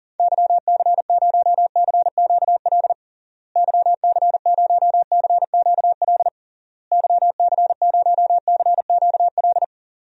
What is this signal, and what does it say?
Morse code, heard as a keyed tone, about 40 wpm, 700 Hz. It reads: YC0CQL YC0CQL YC0CQL
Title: DXCC-100 Call Signs - 6 Characters - Repeated 3x - 40wpm